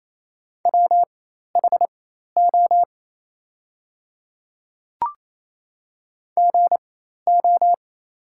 Morse code, heard as a keyed tone, about 28 wpm, spaced as for 13 wpm, 700 Hz. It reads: WHO GO
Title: 4x Character Spacing - Top 100 Words - 28wpm